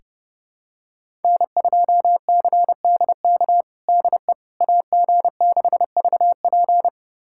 Morse code, heard as a keyed tone, about 30 wpm, 700 Hz. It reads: N2CDK DE AG6VP
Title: USA Callsign DE Callsign - 5 Characters - 30wpm